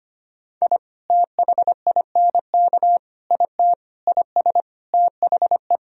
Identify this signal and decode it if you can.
Morse code, heard as a keyed tone, about 25 wpm, 700 Hz. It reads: I THINK IT IS THE
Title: Sentences from Top 1000 Words - Review Part 2 - 25wpm